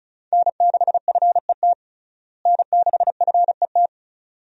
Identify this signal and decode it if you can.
Morse code, heard as a keyed tone, about 34 wpm, 700 Hz. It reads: N6FET N6FET
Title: USA Call Signs - 5 Characters - Repeated 2x - 35wpm